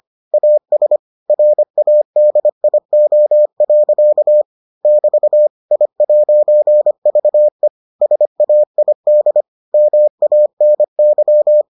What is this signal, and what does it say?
Morse code, heard as a keyed tone, about 25 wpm, 600 Hz. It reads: AS RADIO. <BT> I'VE SAID MANY